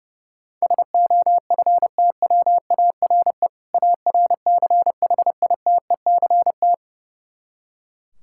Morse code, heard as a keyed tone, about 32 wpm, 700 Hz. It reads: SOFTWARE ARCHITECT
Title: Practice QSO - Exchange 3 - More Info - 30wpm